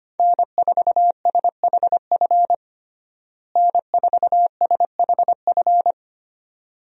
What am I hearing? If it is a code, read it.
Morse code, heard as a keyed tone, about 25 wpm, 700 Hz. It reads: N4SHF N4SHF
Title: USA Call Signs - 5 Characters - Repeated 2x - 25wpm